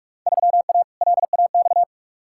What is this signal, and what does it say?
Morse code, heard as a keyed tone, about 46 wpm, 700 Hz. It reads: 3A LAX
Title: Field Day Exchanges - 45wpm